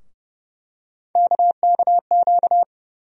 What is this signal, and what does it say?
Morse code, heard as a keyed tone, about 30 wpm, 700 Hz. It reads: KKQ
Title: USA Call Sign Postfixes - 3 Letters 30wpm